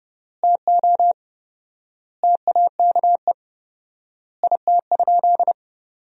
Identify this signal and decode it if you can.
Morse code, heard as a keyed tone, about 30 wpm, 700 Hz. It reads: TO TAKE IT?
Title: 4x Word-Spacing - Sentences from Top 300 Words - 30wpm